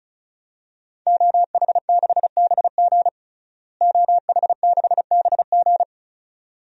Morse code, heard as a keyed tone, about 36 wpm, 700 Hz. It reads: OH6BG OH6BG
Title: DXCC-100 Call Signs - 5 Characters - Repeated 2x - 35wpm